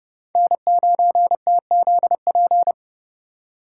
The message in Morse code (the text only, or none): N9TZP